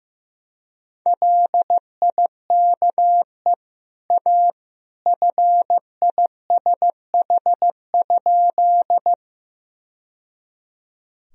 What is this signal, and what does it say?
Morse code, heard as a keyed tone, about 15 wpm, 700 Hz. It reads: LIKE A FISH?
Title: Sentences from Top 1000 Words - Review Part 1 - 15wpm